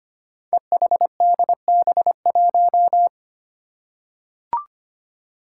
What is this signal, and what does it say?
Morse code, heard as a keyed tone, about 25 wpm, 700 Hz. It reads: EHDB1
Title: Five Challenging Letter-Number Combinations 25wpm